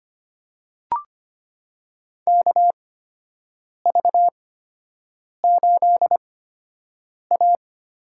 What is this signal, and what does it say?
Morse code, heard as a keyed tone, about 25 wpm, spaced as for 7 wpm, 700 Hz. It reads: KV8A
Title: USA Call Signs - 4 Characters - 8x Character Spacing - 25wpm